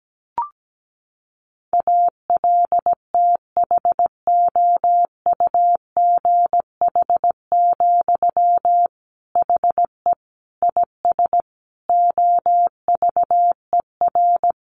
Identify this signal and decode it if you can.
Morse code, heard as a keyed tone, about 17 wpm, 700 Hz. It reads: ALTHOUGH, HE IS OVER
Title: Sentences from Top 600 Words 17wpm